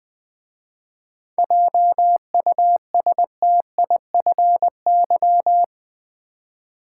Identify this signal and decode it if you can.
Morse code, heard as a keyed tone, about 20 wpm, 700 Hz. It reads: JUSTIFY
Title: 7 Letter Words - First 3 Letters Given - 20wpm